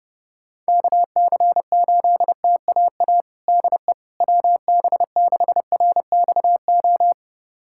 Morse code, heard as a keyed tone, about 32 wpm, 700 Hz. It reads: KC8TAA DE WB6RXO
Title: USA Callsign DE Callsign - 6 Characters - 30wpm